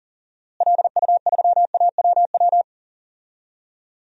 Morse code, heard as a keyed tone, about 37 wpm, 700 Hz. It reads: LU3AWW
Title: DXCC-100 Call Signs - 6 Characters - Repeated 2x - 40wpm